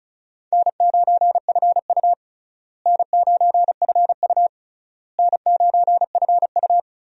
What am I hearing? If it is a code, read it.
Morse code, heard as a keyed tone, about 36 wpm, 700 Hz. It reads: N9FU N9FU N9FU
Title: USA Call Signs - 4 Characters - Repeated 3x - 35wpm